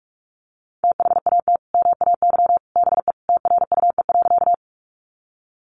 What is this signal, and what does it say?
Morse code, heard as a keyed tone, about 44 wpm, 700 Hz. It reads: THAT MAY BE TRUE.
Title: Sentences from Top 400 Words 45wpm